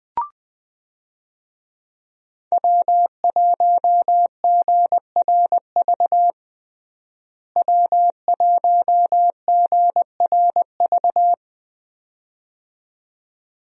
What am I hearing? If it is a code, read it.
Morse code, heard as a keyed tone, about 21 wpm, 700 Hz. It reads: W1GRV W1GRV
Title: USA Call Signs - 5 Characters - Repeated 2x - 20wpm